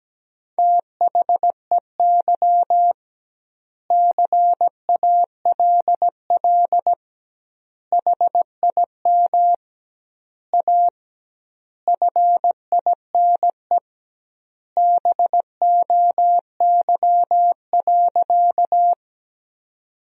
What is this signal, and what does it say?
Morse code, heard as a keyed tone, about 17 wpm, 700 Hz. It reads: THEY CALL HIM A FINE BOY.
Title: 2x Word-Spacing - Sentences from Top 500 Words - 17wpm